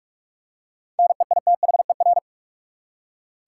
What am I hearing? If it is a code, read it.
Morse code, heard as a keyed tone, about 46 wpm, 700 Hz. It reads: NEITHER
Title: Seven Letter Words - 45wpm